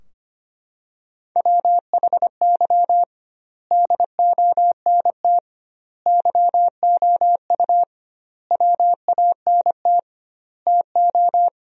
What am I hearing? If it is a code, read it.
Morse code, heard as a keyed tone, about 25 wpm, 700 Hz. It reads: WHY DONT YOU WANT TO
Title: 2x Word-Spacing - Sentences from Top 300 Words - 25wpm